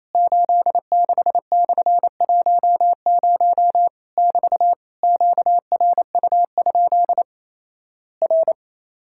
Morse code, heard as a keyed tone, about 28 wpm, 700 Hz. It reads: 86/10 <BT> QRU? R
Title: Worked All Europe DX Contest - QTC Exchange - 28wpm